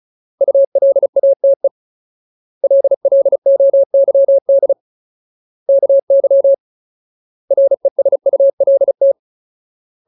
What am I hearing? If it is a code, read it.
Morse code, heard as a keyed tone, about 34 wpm, 550 Hz. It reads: ULATE LLOYD KY RESULT
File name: Ultimate Word List / 35WPM Ultimate